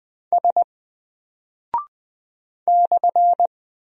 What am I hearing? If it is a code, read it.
Morse code, heard as a keyed tone, about 20 wpm, 700 Hz. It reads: S /
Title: Single Character (Letter:Number:Punctuation:Prosign) 20wpm